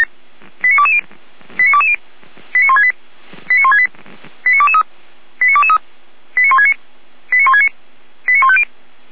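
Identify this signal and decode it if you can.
Five-tone selective call (ZVEI), heard as quick runs of five tones, tones of about 70 ms. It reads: 89220 89220 89168 89168 89223 89223 89169 89169 89150
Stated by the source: ZVEI five-tone call received by SDR, from the audio server of the Feuerwehr Orlamünde fire brigade.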